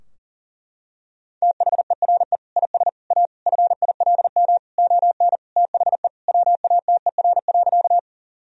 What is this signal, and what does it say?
Morse code, heard as a keyed tone, about 40 wpm, 700 Hz. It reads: THERE IS A FILM ON THE WATER.
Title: Sentences from Top 1000 Words - Review Part 3 - 40wpm